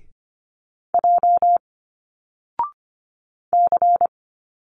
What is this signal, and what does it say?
Morse code, heard as a keyed tone, about 25 wpm, 700 Hz. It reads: J C
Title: Single Letters 25wpm